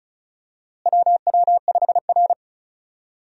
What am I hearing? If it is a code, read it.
Morse code, heard as a keyed tone, about 36 wpm, 700 Hz. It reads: WW5R
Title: USA Call Signs - 4 Characters - Repeated 2x - 35wpm